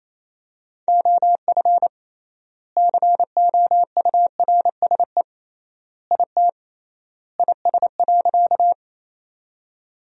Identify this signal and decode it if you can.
Morse code, heard as a keyed tone, about 27 wpm, 700 Hz. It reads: OF COURSE IT IS.
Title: 3x Word-Spacing - Sentences from Top 500 Words - 28wpm